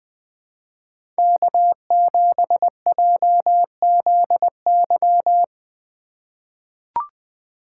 Morse code, heard as a keyed tone, about 20 wpm, 700 Hz. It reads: K7JZY
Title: USA Call Signs - 2x Character Spacing - 20wpm